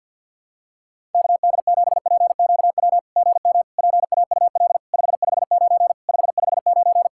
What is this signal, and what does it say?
Morse code, heard as a keyed tone, about 50 wpm, 700 Hz. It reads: KD7PQW GM PAUL 559 559
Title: SOTA QSO Practice 50wpm